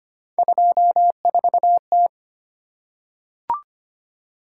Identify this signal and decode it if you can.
Morse code, heard as a keyed tone, about 25 wpm, 700 Hz. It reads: 24T
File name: Three Letter-Number Combinations - 25wpm